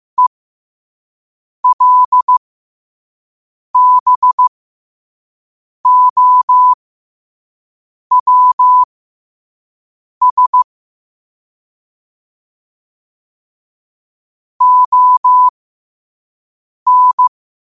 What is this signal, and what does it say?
Morse code, heard as a keyed tone, about 15 wpm, spaced as for 5 wpm, 1000 Hz. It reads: ELBOWS ON